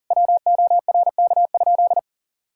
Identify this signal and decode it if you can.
Morse code, heard as a keyed tone, about 40 wpm, 700 Hz. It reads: WORK?
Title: Sentences from Top 1000 Words - Review Part 5 - 40wpm